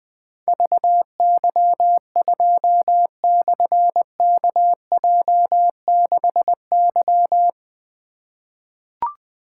Morse code, heard as a keyed tone, about 20 wpm, 700 Hz. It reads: VY2/KJ6Y
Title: Difficult Call Signs 20wpm